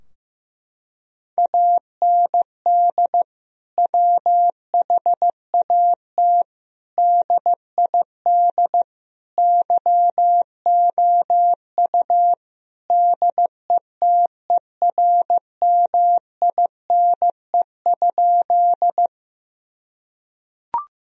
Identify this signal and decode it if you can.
Morse code, heard as a keyed tone, about 15 wpm, 700 Hz. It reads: AND WHAT DID YOU DETERMINE?
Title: Sentences from Top 1000 Words - Review Part 4 - 15wpm